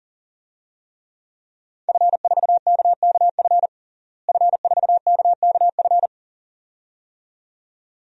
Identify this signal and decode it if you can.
Morse code, heard as a keyed tone, about 40 wpm, 700 Hz. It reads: F4KKF F4KKF
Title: DXCC-100 Call Signs - 5 Characters - Repeated 2x - 40wpm